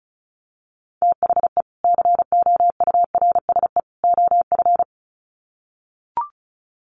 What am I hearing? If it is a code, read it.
Morse code, heard as a keyed tone, about 34 wpm, 700 Hz. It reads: THE COURSE OF
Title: Sets of 3 Words 35wpm